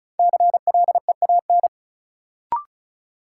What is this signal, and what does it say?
Morse code, heard as a keyed tone, about 36 wpm, 700 Hz. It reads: CLEAN
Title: Sentences from Top 1100 Words - New Word Drill Alt - 35wpm